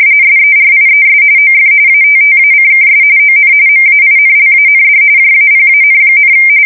RTTY, about 45.45 baud. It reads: SERIES OF ATTACKS ON MAINLAND JAPAN,
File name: fsk170 / USS-Massachusetts